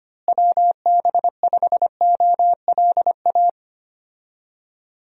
Morse code, heard as a keyed tone, about 25 wpm, 700 Hz. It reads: WB5OLA
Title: USA Call Signs - 6 Characters - Repeated 2x - 25wpm